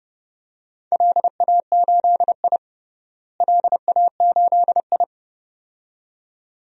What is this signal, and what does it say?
Morse code, heard as a keyed tone, about 32 wpm, 700 Hz. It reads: LA8I LA8I
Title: DXCC-100 Call Signs - 4 Characters - Repeated 2x - 30wpm